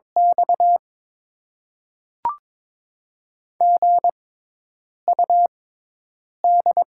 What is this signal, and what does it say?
Morse code, heard as a keyed tone, about 22 wpm, spaced as for 8 wpm, 700 Hz. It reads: X GUD
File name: Top 20 QSO Elements - 6x Character Spacing - 22wpm